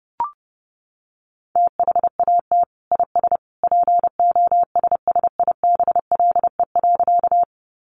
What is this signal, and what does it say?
Morse code, heard as a keyed tone, about 30 wpm, 700 Hz. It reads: THAT IS POSSIBLE.
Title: Sentences from Top 500 Words 30wpm